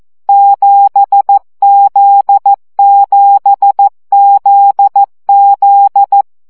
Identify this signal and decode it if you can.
Morse code, heard as a keyed tone, about 14 wpm, 800 Hz. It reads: 7Z7ZZ
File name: CW gyakorlas z 7 vesszo 12 baud